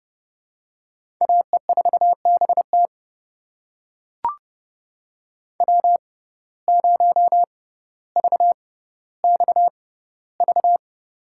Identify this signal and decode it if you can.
Morse code, heard as a keyed tone, about 30 wpm, spaced as for 15 wpm, 700 Hz. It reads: AE4BT W0VXV
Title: USA Call Signs - 5 Characters - 6x Character Spacing - 30wpm